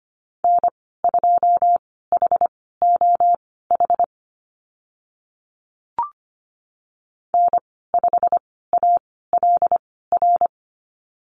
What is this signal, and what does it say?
Morse code, heard as a keyed tone, about 25 wpm, spaced as for 16 wpm, 700 Hz. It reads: N2HOH N5ALR
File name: Lesson 15g - Number 5 - Call Signs 25wpm